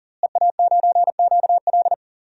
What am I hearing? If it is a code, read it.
Morse code, heard as a keyed tone, about 40 wpm, 700 Hz. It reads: EA9QL
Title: DXCC-100 Call Signs - 5 Characters - Repeated 3x - 40wpm